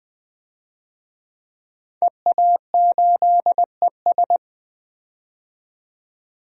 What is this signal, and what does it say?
Morse code, heard as a keyed tone, about 20 wpm, 700 Hz. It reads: EA8ES
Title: DXCC-100 Call Signs - 5 Characters - 20wpm